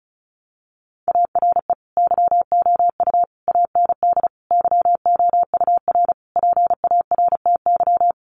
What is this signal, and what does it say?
Morse code, heard as a keyed tone, about 34 wpm, 700 Hz. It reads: ARE YOU AND YOUR PARTY
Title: Sentences from Top 1400 Words 35wpm